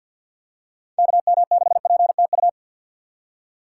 Morse code, heard as a keyed tone, about 51 wpm, 700 Hz. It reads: KM6PTU
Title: USA Call Signs - 6 Characters - 6x Character Spacing - 50wpm